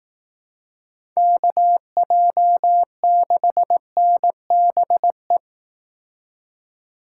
Morse code, heard as a keyed tone, about 18 wpm, 700 Hz. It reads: KJ6NBE
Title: Speed Racing - USA 2x3 Call Signs 15wpm